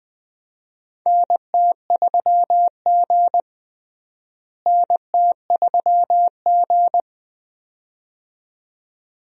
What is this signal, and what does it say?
Morse code, heard as a keyed tone, about 20 wpm, 700 Hz. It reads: NT3G NT3G
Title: USA Call Signs - 4 Characters - Repeated 2x - 20wpm